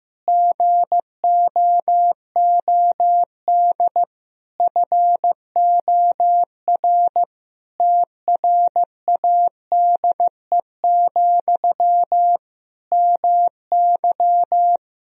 Morse code, heard as a keyed tone, about 15 wpm, 700 Hz. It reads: GOOD FOR TRADE, MY
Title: Sentences from Top 1000 Words - Review Part 4 - 15wpm